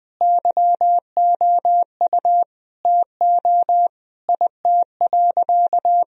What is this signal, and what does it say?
Morse code, heard as a keyed tone, about 20 wpm, 700 Hz. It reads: YOU TO IT.
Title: Sentences from Top 1000 Words - Review Part 2 - 20wpm